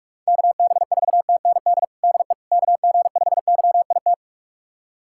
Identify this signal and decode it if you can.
Morse code, heard as a keyed tone, about 41 wpm, 700 Hz. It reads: KB4TND DE KG5YIT
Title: USA Callsign DE Callsign - 6 Characters - 45wpm